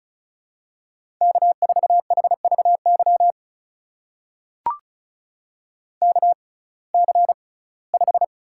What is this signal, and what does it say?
Morse code, heard as a keyed tone, about 36 wpm, spaced as for 21 wpm, 700 Hz. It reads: K4HVY KC5
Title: USA Call Signs - 6x Character Spacing - 35wpm